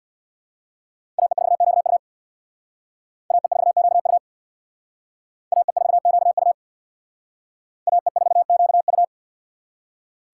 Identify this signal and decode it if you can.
Morse code, heard as a keyed tone, about 65 wpm, 700 Hz. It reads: AE4QU AE4QU AE4QU AE4QU